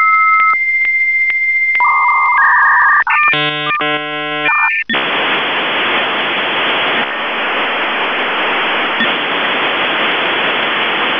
Dial-up modem handshake.
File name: modem / v34bis4